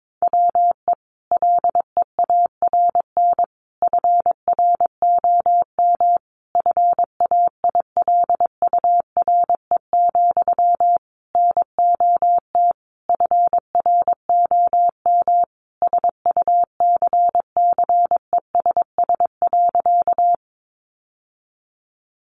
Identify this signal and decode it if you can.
Morse code, heard as a keyed tone, about 22 wpm, 700 Hz. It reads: WE LEARN FROM FAILURE, NOT FROM SUCCESS.
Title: Sentences from Top 900 Words 22wpm